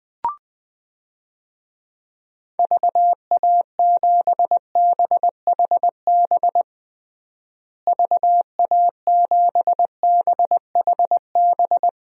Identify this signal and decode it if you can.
Morse code, heard as a keyed tone, about 20 wpm, 700 Hz. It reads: VA7BHB VA7BHB
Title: DXCC-100 Call Signs - 6 Characters - Repeated 3x - 20wpm